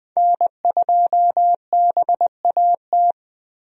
Morse code, heard as a keyed tone, about 21 wpm, 700 Hz. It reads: N2BAT
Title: USA Call Signs - 5 Characters - Repeated 2x - 20wpm